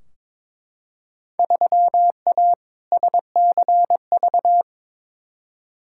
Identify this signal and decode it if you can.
Morse code, heard as a keyed tone, about 22 wpm, 700 Hz. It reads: 3A SCV
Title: Field Day Exchanges - 22wpm